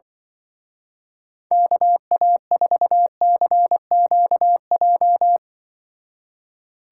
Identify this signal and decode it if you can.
Morse code, heard as a keyed tone, about 23 wpm, 700 Hz. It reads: KA4CQJ
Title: Speed Racing - USA 2x3 Call Signs 20wpm